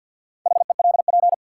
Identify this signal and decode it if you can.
Morse code, heard as a keyed tone, about 53 wpm, 700 Hz. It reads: HELP
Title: Sets of 2 Words - New Word Drill - 50wpm